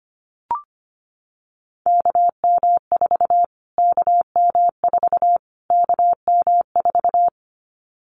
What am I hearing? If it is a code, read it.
Morse code, heard as a keyed tone, about 25 wpm, 700 Hz. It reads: KM4 KM4 KM4
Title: Callsign Prefixes for Top 25 DXCC - Repeated 3x - 25wpm